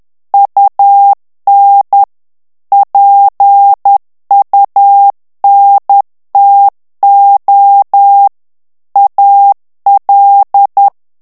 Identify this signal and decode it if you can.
Morse code, heard as a keyed tone, about 10 wpm, 800 Hz. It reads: UN PUNTO AL
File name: ITALIEN texte clair 2 vitesse 10 mots minute